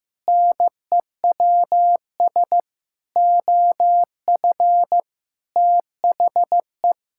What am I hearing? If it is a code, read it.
Morse code, heard as a keyed tone, about 15 wpm, 700 Hz. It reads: NEWS OF THE
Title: Sentences from Top 1600 Words 15wpm